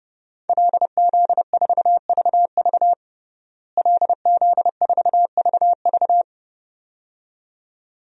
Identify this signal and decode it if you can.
Morse code, heard as a keyed tone, about 30 wpm, 700 Hz. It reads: LZ4VV LZ4VV
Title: DXCC-100 Call Signs - 5 Characters - Repeated 2x - 30wpm